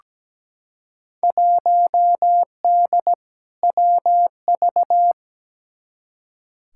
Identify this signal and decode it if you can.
Morse code, heard as a keyed tone, about 17 wpm, 700 Hz. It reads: 1D WV
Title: Field Day Exchanges - 17wpm